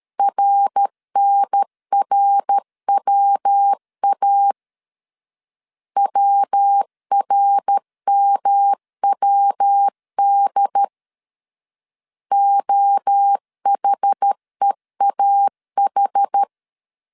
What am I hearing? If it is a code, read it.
Morse code, heard as a keyed tone, about 13 wpm, 800 Hz. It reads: RNRWA WRMWD OHEAH
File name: CW / DET / 1 3 1500C S C 12wpm 2750